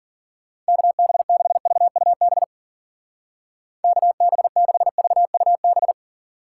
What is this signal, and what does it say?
Morse code, heard as a keyed tone, about 52 wpm, 700 Hz. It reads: KB6VUB KB6VUB